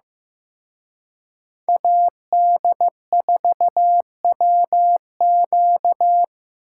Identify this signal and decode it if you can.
Morse code, heard as a keyed tone, about 18 wpm, 700 Hz. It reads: AD4WQ
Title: Speed Racing - USA Call Signs 15wpm